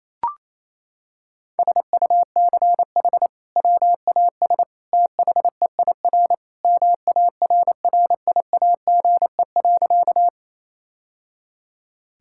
Morse code, heard as a keyed tone, about 28 wpm, 700 Hz. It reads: SUCH WAS THEIR MARRIAGE.
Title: Sentences from Top 1000 Words - Review Part 5 - 28wpm